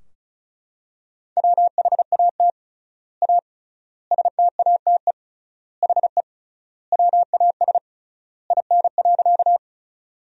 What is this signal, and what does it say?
Morse code, heard as a keyed tone, about 36 wpm, 700 Hz. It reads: WHAT A STATE HE WAS IN.
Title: 3x Word-Spacing - Sentences from Top 400 Words - 35wpm